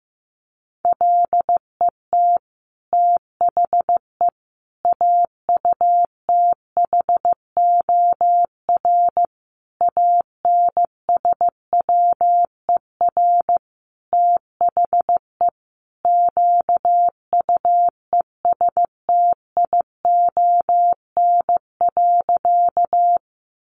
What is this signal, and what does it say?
Morse code, heard as a keyed tone, about 15 wpm, 700 Hz. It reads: LET THE AUTHOR ANSWER THE QUESTION.
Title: Sentences from Top 1200 Words 15wpm